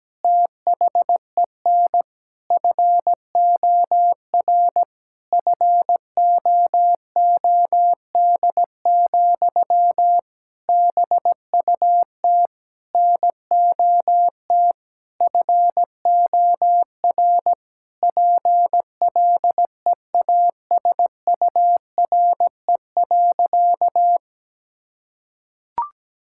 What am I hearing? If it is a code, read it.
Morse code, heard as a keyed tone, about 17 wpm, 700 Hz. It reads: THEN FOR FOOD, BUT NOT FOR PLEASURE.
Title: Sentences from Top 1600 Words 17wpm